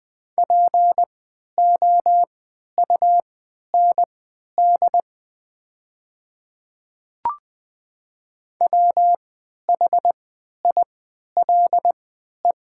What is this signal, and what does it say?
Morse code, heard as a keyed tone, about 20 wpm, spaced as for 11 wpm, 700 Hz. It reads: POUND WHILE
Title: 3x Character Spacing - Top 500 Words - 20wpm